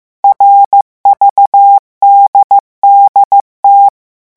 Morse code, heard as a keyed tone, about 15 wpm, 800 Hz. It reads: RVDDT
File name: morse mp3 / revision11 lettres vitesse 14